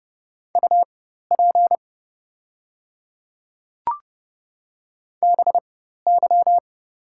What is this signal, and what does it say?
Morse code, heard as a keyed tone, about 30 wpm, spaced as for 14 wpm, 700 Hz. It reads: UP BY
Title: 4x Character Spacing - Top 100 Words - 30wpm